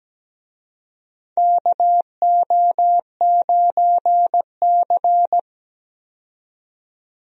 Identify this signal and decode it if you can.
Morse code, heard as a keyed tone, about 17 wpm, 700 Hz. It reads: KO9C
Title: USA Call Signs - 4 Characters - 17wpm